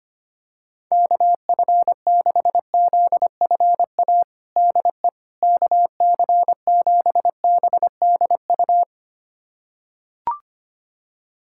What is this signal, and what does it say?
Morse code, heard as a keyed tone, about 25 wpm, 700 Hz. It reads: KF6ZFA DE KC7BDU
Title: USA Callsign DE Callsign - 6 Characters - 25wpm